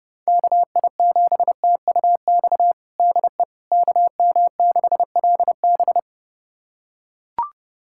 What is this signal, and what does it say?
Morse code, heard as a keyed tone, about 30 wpm, 700 Hz. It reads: KI7TUX DE KM6LB
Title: USA Callsign DE Callsign 30wpm